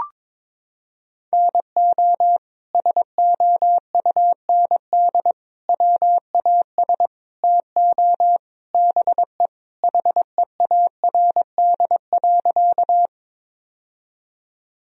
Morse code, heard as a keyed tone, about 22 wpm, 700 Hz. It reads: NO SOUND WAS TO BE HEARD.
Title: Sentences from Top 1000 Words - Review Part 1 - 22wpm